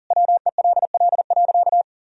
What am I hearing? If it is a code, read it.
Morse code, heard as a keyed tone, about 40 wpm, 700 Hz. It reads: WELL.
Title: Sentences from Top 1000 Words - Review Part 4 - 40wpm